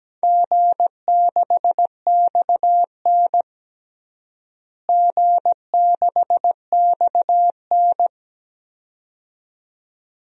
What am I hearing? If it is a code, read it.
Morse code, heard as a keyed tone, about 17 wpm, 700 Hz. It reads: G6XN G6XN
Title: DXCC-100 Call Signs - 4 Characters - Repeated 2x - 17wpm